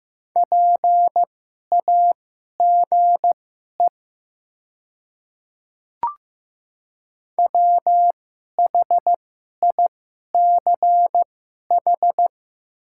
Morse code, heard as a keyed tone, about 15 wpm, spaced as for 11 wpm, 700 Hz. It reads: PAGE WHICH
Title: 2x Character Spacing - Top 300 Words - 15wpm